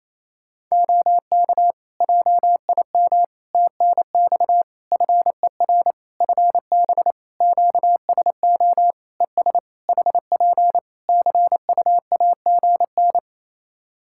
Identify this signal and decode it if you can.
Morse code, heard as a keyed tone, about 28 wpm, 700 Hz. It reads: OK JIM TNX FER FB QSO ES HP CUAGN
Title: Practice QSO - Exchange 4 - Ending - 28wpm